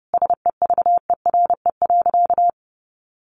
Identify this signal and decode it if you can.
Morse code, heard as a keyed tone, about 30 wpm, 700 Hz. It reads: SEVERE.
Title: Sentences from Top 1400 Words 30wpm